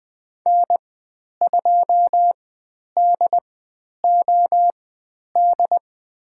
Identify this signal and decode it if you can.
Morse code, heard as a keyed tone, about 20 wpm, spaced as for 10 wpm, 700 Hz. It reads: N2DOD
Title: 102 Lesson 17g - Letter u - Call Signs 20wpm